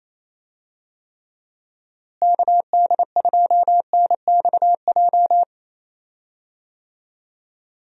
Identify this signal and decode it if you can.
Morse code, heard as a keyed tone, about 28 wpm, 700 Hz. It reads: KD2NXJ
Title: USA Call Signs - 6 Characters - Repeated 2x - 28wpm